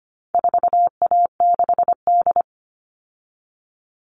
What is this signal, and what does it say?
Morse code, heard as a keyed tone, about 25 wpm, 700 Hz. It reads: 4A6D
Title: All Cut Numbers - 4-Digit 25wpm